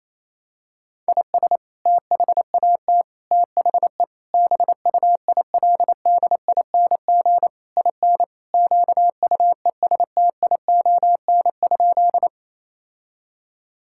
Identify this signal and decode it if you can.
Morse code, heard as a keyed tone, about 28 wpm, 700 Hz. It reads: IS THAT THE BUILDING IN QUESTION?
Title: Sentences from Top 1000 Words - Review Part 5 - 28wpm